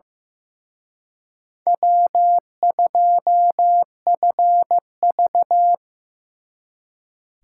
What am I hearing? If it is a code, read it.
Morse code, heard as a keyed tone, about 18 wpm, 700 Hz. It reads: W2FV
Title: Speed Racing - USA Call Signs 15wpm